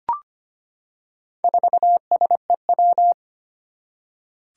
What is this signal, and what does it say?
Morse code, heard as a keyed tone, about 25 wpm, 700 Hz. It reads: 4SEW
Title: Four Challenging Letter-Number Combinations 25wpm